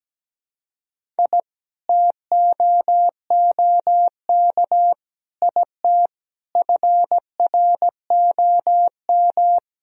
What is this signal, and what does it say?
Morse code, heard as a keyed tone, about 17 wpm, 700 Hz. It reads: I TOOK IT FROM
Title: Sentences from Top 1000 Words - Review Part 1 - 17wpm